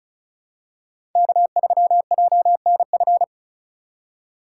Morse code, heard as a keyed tone, about 36 wpm, 700 Hz. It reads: K3JNF
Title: USA Call Signs - 5 Characters - 2x Character Spacing - 35wpm